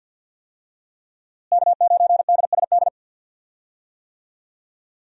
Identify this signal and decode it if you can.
Morse code, heard as a keyed tone, about 46 wpm, 700 Hz. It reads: K9DSD
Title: USA Call Signs - 5 Characters - Repeated 3x - 50wpm